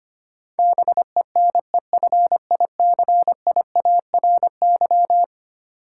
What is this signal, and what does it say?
Morse code, heard as a keyed tone, about 25 wpm, 700 Hz. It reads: BENEFICIARY
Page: morse-code-ninja-practice-25wpm